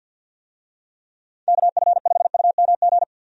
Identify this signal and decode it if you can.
Morse code, heard as a keyed tone, about 46 wpm, 700 Hz. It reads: KF5UMG
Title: USA Call Signs - 6 Characters - 50wpm